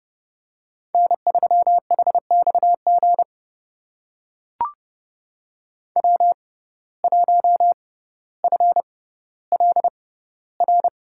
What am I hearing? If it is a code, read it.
Morse code, heard as a keyed tone, about 32 wpm, spaced as for 16 wpm, 700 Hz. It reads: N3HXG W1FLR
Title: USA Call Signs - 5 Characters - 6x Character Spacing - 30wpm